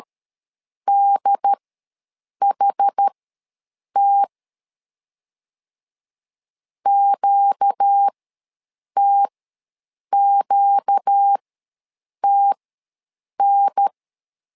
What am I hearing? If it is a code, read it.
Morse code, heard as a keyed tone, about 13 wpm, spaced as for 7 wpm, 800 Hz. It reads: DHT QTQTN
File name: LEZIONI / 4 Q ws 12wpm 1912